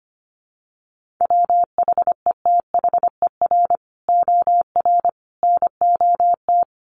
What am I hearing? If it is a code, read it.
Morse code, heard as a keyed tone, about 25 wpm, 700 Hz. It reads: WHETHER OR NOT
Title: Sets of 3 Words 25wpm